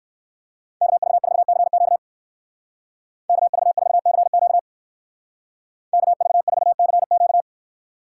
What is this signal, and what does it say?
Morse code, heard as a keyed tone, about 58 wpm, 700 Hz. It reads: KV4CQ KV4CQ KV4CQ